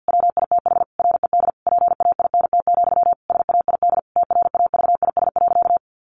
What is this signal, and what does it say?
Morse code, heard as a keyed tone, about 50 wpm, 700 Hz. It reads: WITH RED PAINT, SAID TRAVIS.
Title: A Sound Of Thunder 50wpm